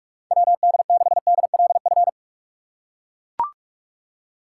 Morse code, heard as a keyed tone, about 43 wpm, 700 Hz. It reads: WD6DLF
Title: USA Call Signs - 2x Character Spacing - 45wpm